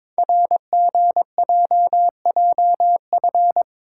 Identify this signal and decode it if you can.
Morse code, heard as a keyed tone, about 22 wpm, 700 Hz. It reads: RGJJF
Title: Code Groups - 5 Letters - 22wpm